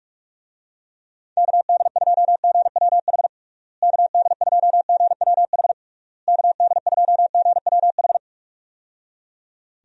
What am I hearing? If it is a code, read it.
Morse code, heard as a keyed tone, about 46 wpm, 700 Hz. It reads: KD2GWH KD2GWH KD2GWH
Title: USA Call Signs - 6 Characters - Repeated 3x - 45wpm